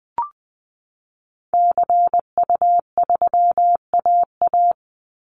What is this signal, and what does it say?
Morse code, heard as a keyed tone, about 20 wpm, 700 Hz. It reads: CU3AA
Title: Call Signs Repeated 3x - 20wpm